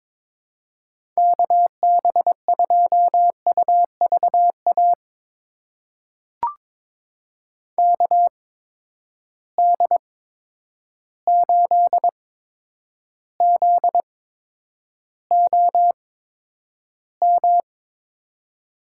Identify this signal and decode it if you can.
Morse code, heard as a keyed tone, about 22 wpm, spaced as for 9 wpm, 700 Hz. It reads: KB2UVA KD8ZOM
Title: USA Call Signs - 8x Character Spacing - 22wpm